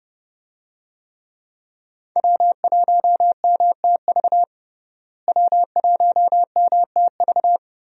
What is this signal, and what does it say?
Morse code, heard as a keyed tone, about 30 wpm, 700 Hz. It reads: W1MTV W1MTV
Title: USA Call Signs - 5 Characters - Repeated 3x - 30wpm